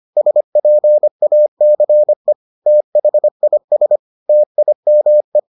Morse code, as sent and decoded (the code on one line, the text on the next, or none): ... .--. .- -.-. / - .... .. ... / - .. -- .
SPACE THIS TIME